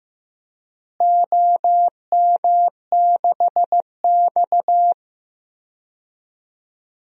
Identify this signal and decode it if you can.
Morse code, heard as a keyed tone, about 15 wpm, 700 Hz. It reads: OM6X
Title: DXCC-100 Call Signs - 4 Characters - Repeated 2x - 15wpm